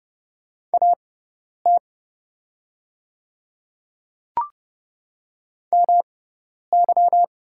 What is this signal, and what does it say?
Morse code, heard as a keyed tone, about 30 wpm, spaced as for 10 wpm, 700 Hz. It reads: AT MY
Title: 6x Character Spacing - Top 100 Words - 30wpm